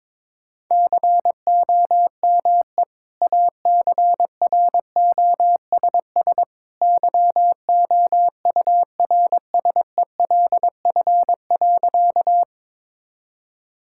Morse code, heard as a keyed tone, about 22 wpm, 700 Hz. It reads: COME ACROSS YOURSELF.
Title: Sentences from Top 1000 Words - Review Part 5 - 22wpm